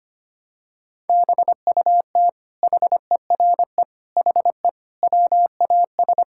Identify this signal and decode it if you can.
Morse code, heard as a keyed tone, about 25 wpm, 700 Hz. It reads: BUT HERE HE WAS
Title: Sentences from Top 1600 Words 25wpm